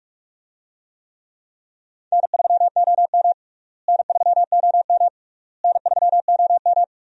Morse code, heard as a keyed tone, about 42 wpm, 700 Hz. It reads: N3OM N3OM N3OM
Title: USA Call Signs - 4 Characters - Repeated 3x - 45wpm